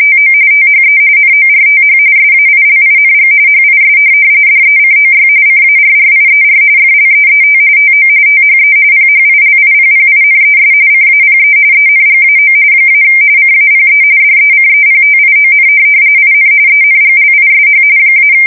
RTTY, about 45.45 baud. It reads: TO THE NORTH SAILING HIS DECOY CARRIER FLEET TO THE PHILIPPINES CAPE ENGAO - CAPE DECEPTION IN ENGLISH -
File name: fsk170 / USS-New-Jersey